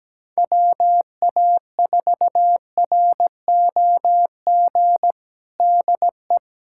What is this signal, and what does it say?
Morse code, heard as a keyed tone, about 17 wpm, 700 Hz. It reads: WA4ROG DE
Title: USA Callsign DE Callsign - 6 Characters - 17wpm